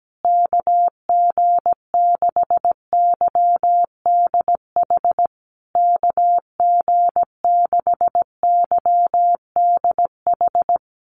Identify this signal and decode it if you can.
Morse code, heard as a keyed tone, about 17 wpm, 700 Hz. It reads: KG6YDH KG6YDH
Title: USA Call Signs Repeated 3x 17wpm